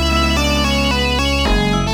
There is no code.